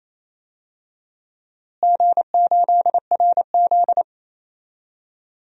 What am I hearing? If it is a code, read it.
Morse code, heard as a keyed tone, about 28 wpm, 700 Hz. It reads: G8RZ
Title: DXCC-100 Call Signs - 4 Characters - Repeated 3x - 28wpm